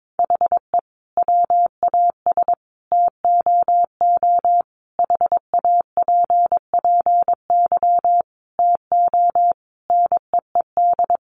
Morse code, heard as a keyed tone, about 22 wpm, 700 Hz. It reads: HE WAS TOO HAPPY TO NEED
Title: Sentences from Top 1000 Words 22wpm